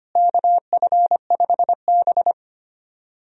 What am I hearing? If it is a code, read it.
Morse code, heard as a keyed tone, about 25 wpm, 700 Hz. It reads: KF5B
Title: USA Call Signs - 4 Characters - Repeated 2x - 25wpm